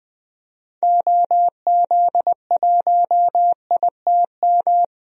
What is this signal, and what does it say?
Morse code, heard as a keyed tone, about 20 wpm, 700 Hz. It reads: OZ1ITM
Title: DXCC-100 Call Signs - 6 Characters - 20wpm